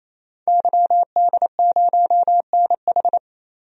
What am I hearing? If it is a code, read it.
Morse code, heard as a keyed tone, about 28 wpm, 700 Hz. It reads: YD0NH
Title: DXCC-100 Call Signs - 5 Characters - 28wpm